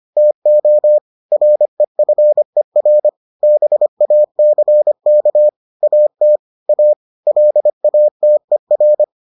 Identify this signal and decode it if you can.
Morse code, heard as a keyed tone, about 25 wpm, 600 Hz. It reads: TO REFER BACK AT A LATER